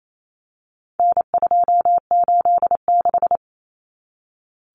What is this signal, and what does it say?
Morse code, heard as a keyed tone, about 28 wpm, 700 Hz. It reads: N286
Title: Common Cut Numbers - 4-Digit 28wpm